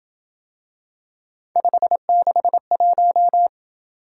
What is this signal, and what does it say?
Morse code, heard as a keyed tone, about 26 wpm, 700 Hz. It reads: 561